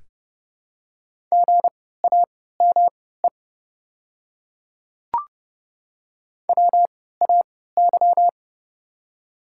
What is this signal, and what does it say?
Morse code, heard as a keyed tone, about 30 wpm, spaced as for 17 wpm, 700 Hz. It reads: GAME WAY
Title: 3x Character Spacing - Top 500 Words - 30wpm